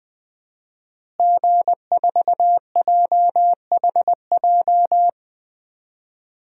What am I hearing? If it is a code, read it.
Morse code, heard as a keyed tone, about 19 wpm, 700 Hz. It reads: G4JHJ
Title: DXCC-100 Call Signs - 5 Characters - Repeated 3x - 20wpm